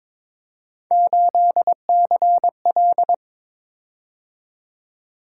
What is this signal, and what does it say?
Morse code, heard as a keyed tone, about 22 wpm, 700 Hz. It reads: 8CL
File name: practice / 22wpm / Three Letter-Number Combinations - 22wpm